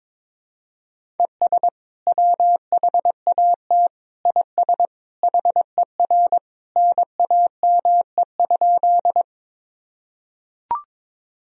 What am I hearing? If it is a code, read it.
Morse code, heard as a keyed tone, about 22 wpm, 700 Hz. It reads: ES WHAT IS HER NAME?
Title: Sentences from Top 200 Words and Abbreviations 22wpm